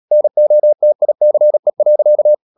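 Morse code, heard as a keyed tone, about 37 wpm, 600 Hz. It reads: NOTICE.
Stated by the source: Morse code practice at 35–39 wpm sending English text of Ken Follett's novel Never, from CW med Gnister.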